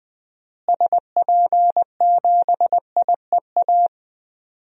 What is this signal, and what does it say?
Morse code, heard as a keyed tone, about 20 wpm, 700 Hz. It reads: SP7IEA
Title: DXCC-100 Call Signs - 6 Characters - Repeated 3x - 20wpm